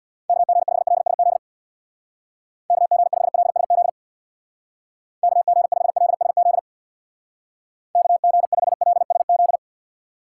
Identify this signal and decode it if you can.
Morse code, heard as a keyed tone, about 60 wpm, 700 Hz. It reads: KG5LSZ KG5LSZ KG5LSZ KG5LSZ
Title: Speed Racing - USA 2x3 Call Signs 50wpm